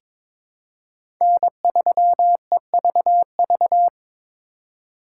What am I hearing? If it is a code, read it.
Morse code, heard as a keyed tone, about 21 wpm, 700 Hz. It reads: N3EVV
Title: USA Call Signs - 5 Characters - 4x Character Spacing - 22wpm